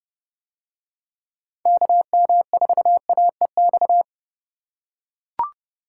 Morse code, heard as a keyed tone, about 30 wpm, 700 Hz. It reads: KM4AEX